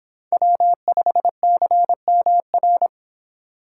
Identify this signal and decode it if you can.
Morse code, heard as a keyed tone, about 31 wpm, 700 Hz. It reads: W5CMR